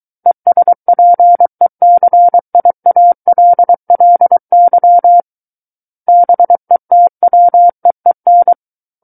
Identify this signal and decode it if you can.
Morse code, heard as a keyed tone, about 23 wpm, 700 Hz. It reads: ESPECIALLY BETWEEN
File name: Book One / 14-Chapter 14 - 23 WPM